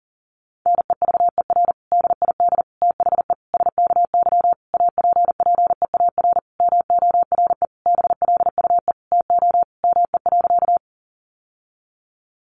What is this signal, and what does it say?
Morse code, heard as a keyed tone, about 40 wpm, 700 Hz. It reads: NEVER DID THE SKY APPEAR MORE BLUE TO ME.
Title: Sentences from Top 1200 Words 40wpm